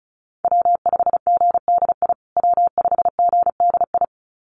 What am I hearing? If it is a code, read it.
Morse code, heard as a keyed tone, about 34 wpm, 700 Hz. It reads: W5GDI W5GDI
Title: USA Call Signs Repeated 2x 35wpm